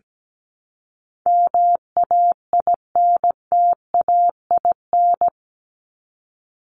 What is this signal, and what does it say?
Morse code, heard as a keyed tone, about 17 wpm, 700 Hz. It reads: MAINTAIN
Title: Top 501-750 Words 17wpm